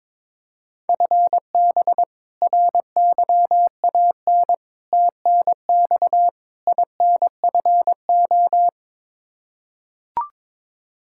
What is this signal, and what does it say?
Morse code, heard as a keyed tone, about 22 wpm, 700 Hz. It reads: FB RYAN TNX INFO
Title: K1USN SST QSO Practice 22wpm